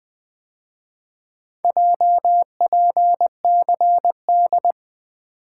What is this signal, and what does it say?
Morse code, heard as a keyed tone, about 24 wpm, 700 Hz. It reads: JPCD
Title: Speed-Racing - Four Letter-Number Combinations - 20wpm